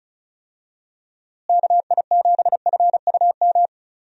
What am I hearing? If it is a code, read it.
Morse code, heard as a keyed tone, about 36 wpm, 700 Hz. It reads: KI7FUM
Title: USA Call Signs - 6 Characters - Repeated 2x - 35wpm